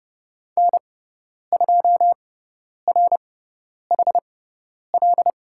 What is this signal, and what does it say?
Morse code, heard as a keyed tone, about 30 wpm, spaced as for 10 wpm, 700 Hz. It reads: N2RHL
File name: Lesson 16g - Letter c - Call Signs 30wpm